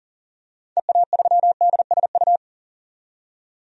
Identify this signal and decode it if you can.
Morse code, heard as a keyed tone, about 40 wpm, 700 Hz. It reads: EA3DSU
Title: DXCC-100 Call Signs - 6 Characters - 40wpm